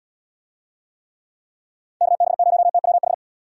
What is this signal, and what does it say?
Morse code, heard as a keyed tone, about 68 wpm, 700 Hz. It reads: KB0TWH